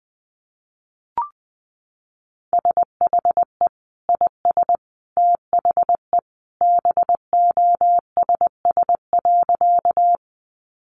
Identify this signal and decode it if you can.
Morse code, heard as a keyed tone, about 20 wpm, 700 Hz. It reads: SHE IS THE BOSS.
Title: Sentences from Top 700 Words 20wpm